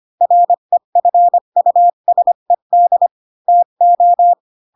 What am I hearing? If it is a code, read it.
Morse code, heard as a keyed tone, about 25 wpm, 700 Hz. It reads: REFUSED TO
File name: Pinocchio / Pinocchio ch3 20 25 ik1hge